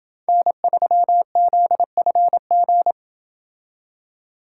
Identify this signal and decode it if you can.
Morse code, heard as a keyed tone, about 28 wpm, 700 Hz. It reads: N3ZFG